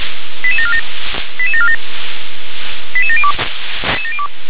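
Five-tone selective call (ZVEI), heard as quick runs of five tones, tones of about 70 ms. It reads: 88747 88747 88702
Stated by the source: ZVEI five-tone call 88740 received on SDR.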